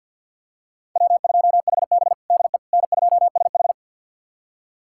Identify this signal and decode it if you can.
Morse code, heard as a keyed tone, about 46 wpm, 700 Hz. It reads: W2HB DE N2SH
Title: USA Callsign DE Callsign - 4 Characters - 50wpm